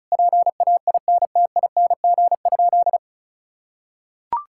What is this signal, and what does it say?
Morse code, heard as a keyed tone, about 36 wpm, 700 Hz. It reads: PAINTING?
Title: Sentences from Top 1000 Words - Review Part 6 - 35wpm